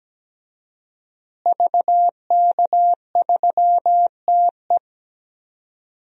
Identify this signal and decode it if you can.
Morse code, heard as a keyed tone, about 17 wpm, 700 Hz. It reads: VK3TE
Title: DXCC-100 Call Signs - 5 Characters - Repeated 3x - 17wpm